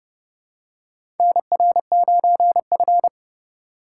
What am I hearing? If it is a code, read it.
Morse code, heard as a keyed tone, about 31 wpm, 700 Hz. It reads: NR9F
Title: USA Call Signs - 4 Characters - 2x Character Spacing - 30wpm